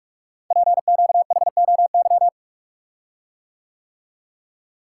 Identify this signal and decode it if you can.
Morse code, heard as a keyed tone, about 45 wpm, 700 Hz. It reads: PQHOY
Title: Code Groups - 5 Letters - 45wpm